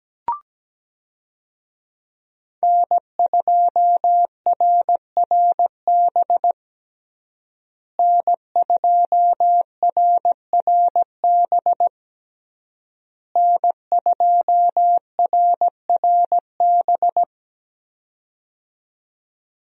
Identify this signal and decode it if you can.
Morse code, heard as a keyed tone, about 17 wpm, 700 Hz. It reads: N2RRB N2RRB N2RRB
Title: USA Call Signs - 5 Characters - Repeated 3x - 17wpm